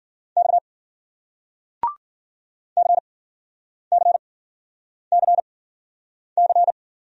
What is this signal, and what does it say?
Morse code, heard as a keyed tone, about 46 wpm, spaced as for 9 wpm, 700 Hz. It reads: X CCCC
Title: Speed-Racing Single Letters 40wpm